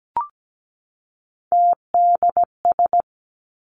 Morse code, heard as a keyed tone, about 17 wpm, 700 Hz. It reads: TDS
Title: Three Challenging Letter-Number Combinations 17wpm